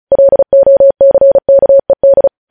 Morse code, heard as a keyed tone, about 36 wpm, 550 Hz. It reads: LOCKED
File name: podcast / News headlines-1x-35x35